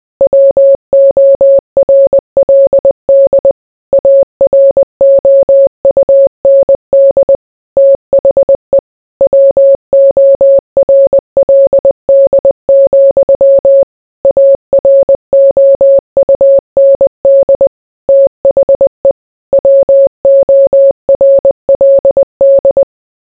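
Morse code, heard as a keyed tone, about 20 wpm, 550 Hz. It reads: WORLD AROUND THE WORLD, AROUND THE WORLD
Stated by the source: Morse code mixed with noise and music as background sound for sleep.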